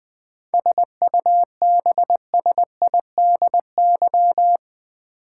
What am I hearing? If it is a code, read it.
Morse code, heard as a keyed tone, about 20 wpm, 700 Hz. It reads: SUBSIDY
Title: 7 Letter Words - First 3 Letters Given - 20wpm